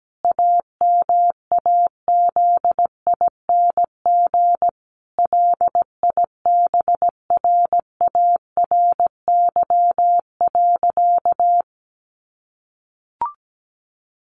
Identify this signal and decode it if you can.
Morse code, heard as a keyed tone, about 17 wpm, 700 Hz. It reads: AMAZING LIBRARY.
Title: Sentences from Top 1500 Words 17wpm